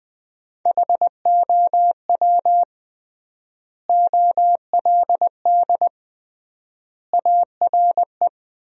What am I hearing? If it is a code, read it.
Morse code, heard as a keyed tone, about 20 wpm, 700 Hz. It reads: HOW OLD ARE
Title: 3x Word-Spacing - Sentences from Top 200 Words - 20wpm